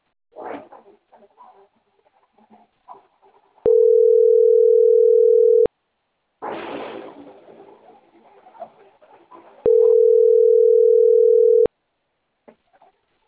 Ringback tone (North America), 440 + 480 Hz.